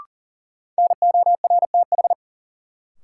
Morse code, heard as a keyed tone, about 37 wpm, 700 Hz. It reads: NORTH
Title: Sentences from Top 400 Words - New Word Drill - 40wpm